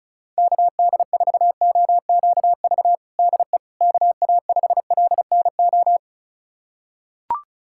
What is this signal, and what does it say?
Morse code, heard as a keyed tone, about 36 wpm, 700 Hz. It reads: KD4OQV DE KA5LNO
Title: USA Callsign DE Callsign - 6 Characters - 35wpm